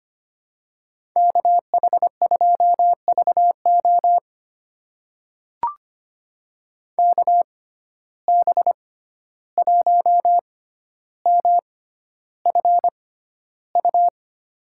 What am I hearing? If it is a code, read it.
Morse code, heard as a keyed tone, about 25 wpm, spaced as for 12 wpm, 700 Hz. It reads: KH2VO KB1MFU
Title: USA Call Signs - 6x Character Spacing - 25wpm